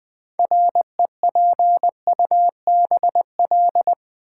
Morse code, heard as a keyed tone, about 20 wpm, 700 Hz. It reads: REPUBL
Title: Build A Word - 3 Letters At A Time - 8-Letter Words - 20wpm